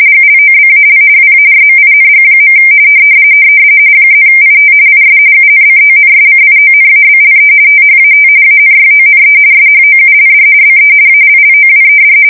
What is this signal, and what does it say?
RTTY, about 45.45 baud. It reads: GLOBAL WARMING. +IT IS UNCONSCIONABLE THAT ANYONE SHOULD BE ALLOWED
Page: 170 Shift 2125/2295